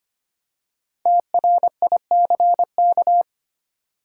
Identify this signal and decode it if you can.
Morse code, heard as a keyed tone, about 25 wpm, 700 Hz. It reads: TRICK
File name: 5 Letter Words - First 3 Letters Given - 25wpm